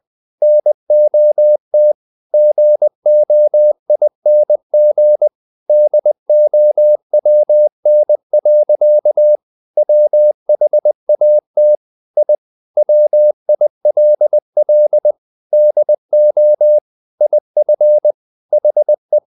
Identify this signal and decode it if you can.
Morse code, heard as a keyed tone, about 20 wpm, 600 Hz. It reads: NOT GOING DOWN. WHAT I WILL DO IF HE
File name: the old man and the sea / the old man and the sea part 1 20wpm